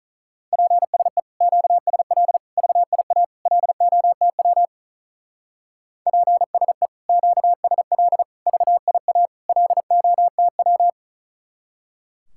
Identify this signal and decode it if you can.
Morse code, heard as a keyed tone, about 44 wpm, 700 Hz. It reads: PSE QSL VIA LOTW PSE QSL VIA LOTW
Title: Speed-Racing Common QSO Phrases Encore - 35wpm